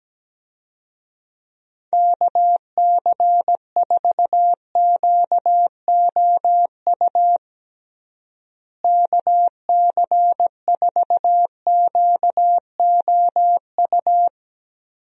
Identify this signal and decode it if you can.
Morse code, heard as a keyed tone, about 17 wpm, 700 Hz. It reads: KC4QOU KC4QOU
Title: USA Call Signs - 6 Characters - Repeated 2x - 17wpm